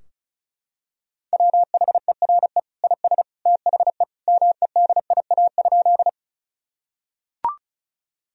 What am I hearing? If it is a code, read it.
Morse code, heard as a keyed tone, about 36 wpm, 700 Hz. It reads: WHERE IS THE MEDIA?
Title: Sentences from Top 1000 Words - Review Part 4 - 35wpm